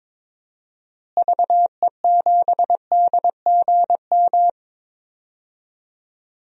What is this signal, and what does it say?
Morse code, heard as a keyed tone, about 21 wpm, 700 Hz. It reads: VE7DGM
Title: DXCC-100 Call Signs - 6 Characters - Repeated 2x - 22wpm